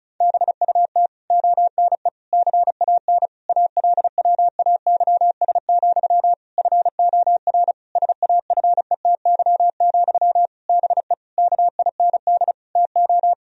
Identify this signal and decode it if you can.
Morse code, heard as a keyed tone, about 36 wpm, 700 Hz. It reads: BUT ONE CAN ALWAYS, FOR SAFETY, BE KIND TO
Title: Sentences from Top 1000 Words - Review Part 4 - 35wpm